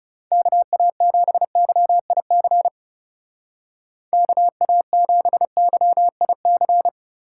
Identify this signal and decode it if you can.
Morse code, heard as a keyed tone, about 37 wpm, 700 Hz. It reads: KA7YIC KA7YIC
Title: Speed Racing - USA 2x3 Call Signs 30wpm